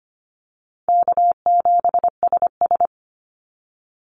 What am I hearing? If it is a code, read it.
Morse code, heard as a keyed tone, about 25 wpm, 700 Hz. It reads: K7SS
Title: Easy Call Signs Repeated 2x - 25wpm